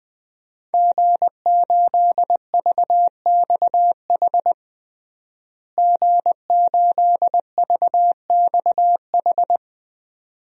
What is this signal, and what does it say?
Morse code, heard as a keyed tone, about 20 wpm, 700 Hz. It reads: G8VXH G8VXH
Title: DXCC-100 Call Signs - 5 Characters - Repeated 2x - 20wpm